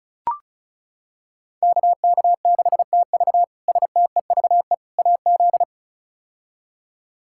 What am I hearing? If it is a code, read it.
Morse code, heard as a keyed tone, about 36 wpm, 700 Hz. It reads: KK6TV STEVE AZ
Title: K1USN SST QSO Practice - 35wpm